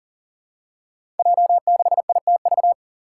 Morse code, heard as a keyed tone, about 40 wpm, 700 Hz. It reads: J6ITV
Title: Five Challenging Letter-Number Combinations 40wpm